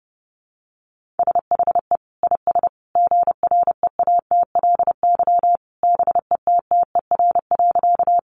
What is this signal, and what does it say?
Morse code, heard as a keyed tone, about 30 wpm, 700 Hz. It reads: SHE IS GREATLY BETTER.
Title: Sentences from Top 1400 Words 30wpm